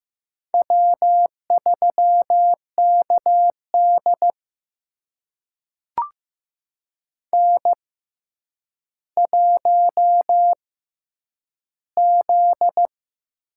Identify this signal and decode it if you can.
Morse code, heard as a keyed tone, about 15 wpm, spaced as for 8 wpm, 700 Hz. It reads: W3KD N1Z
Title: USA Call Signs - 4 Characters - 6x Character Spacing - 15wpm